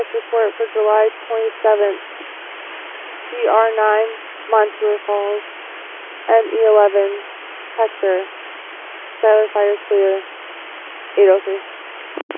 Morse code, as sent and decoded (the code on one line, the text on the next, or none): none